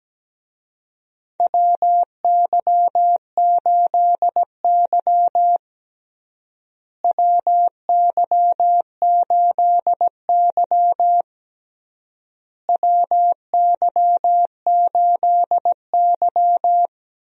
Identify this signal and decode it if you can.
Morse code, heard as a keyed tone, about 17 wpm, 700 Hz. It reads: WY8Y WY8Y WY8Y
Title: USA Call Signs - 4 Characters - Repeated 3x - 17wpm